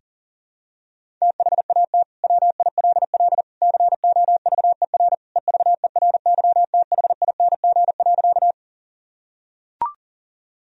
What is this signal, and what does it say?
Morse code, heard as a keyed tone, about 40 wpm, 700 Hz. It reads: THAT WILL COVER EVERYTHING.
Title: Sentences from Top 1000 Words - Review Part 1 - 40wpm